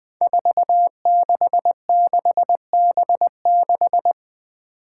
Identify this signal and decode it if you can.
Morse code, heard as a keyed tone, about 20 wpm, 700 Hz. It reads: 466B6
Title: Five Challenging Letter-Number Combinations 20wpm